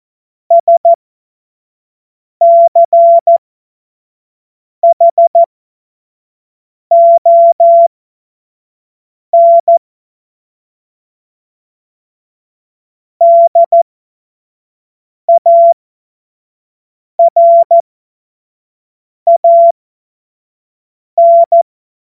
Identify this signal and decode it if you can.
Morse code, heard as a keyed tone, about 14 wpm, spaced as for 5 wpm, 680 Hz. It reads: SCHON DARAN
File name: Morsekurs / Witze / Witz 17. Januar 2023 25 F70